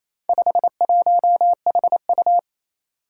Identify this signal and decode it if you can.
Morse code, heard as a keyed tone, about 28 wpm, 700 Hz. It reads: 51HU
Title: Four Challenging Letter-Number Combinations 28wpm